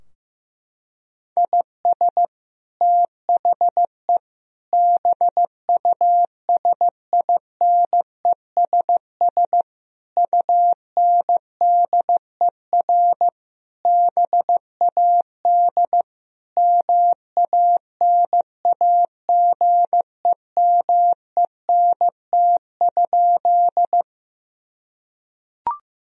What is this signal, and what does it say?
Morse code, heard as a keyed tone, about 15 wpm, 700 Hz. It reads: IS THE BUSINESS UNDER BAD MANAGEMENT?
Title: Sentences from Top 1000 Words - Review Part 4 - 15wpm